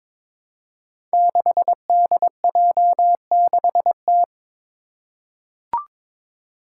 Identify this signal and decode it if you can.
Morse code, heard as a keyed tone, about 22 wpm, 700 Hz. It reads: 6DJ6T
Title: Five Challenging Letter-Number Combinations 22wpm